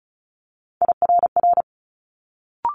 Morse code, heard as a keyed tone, about 36 wpm, 700 Hz. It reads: IRR
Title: Multi-letter Phonemes Quick Fire 35wpm